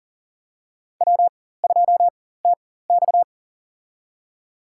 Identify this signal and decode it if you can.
Morse code, heard as a keyed tone, about 40 wpm, spaced as for 19 wpm, 700 Hz. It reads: W2TX
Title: USA Call Signs - 4 Characters - 4x Character Spacing - 40wpm